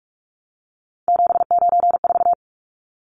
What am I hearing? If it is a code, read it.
Morse code, heard as a keyed tone, about 44 wpm, 700 Hz. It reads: 794
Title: Numbers - 3-Digit 45wpm